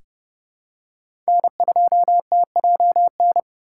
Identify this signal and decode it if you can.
Morse code, heard as a keyed tone, about 30 wpm, 700 Hz. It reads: N2TJN
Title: USA Call Signs - 5 Characters - 2x Character Spacing - 30wpm